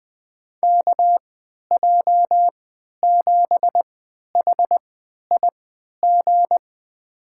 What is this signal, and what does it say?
Morse code, heard as a keyed tone, about 20 wpm, spaced as for 11 wpm, 700 Hz. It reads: KJ7HIG
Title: USA Call Signs - 6 Characters - 3x Character Spacing - 20wpm